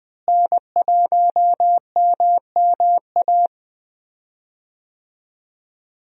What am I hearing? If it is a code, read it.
Morse code, heard as a keyed tone, about 20 wpm, 700 Hz. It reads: N1MMA
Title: USA Call Signs - 5 Characters - Repeated 2x - 20wpm